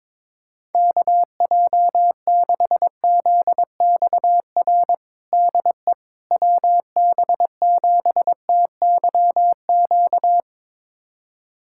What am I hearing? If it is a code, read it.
Morse code, heard as a keyed tone, about 22 wpm, 700 Hz. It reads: KJ6ZXR DE WB7TYQ
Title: USA Callsign DE Callsign - 6 Characters - 22wpm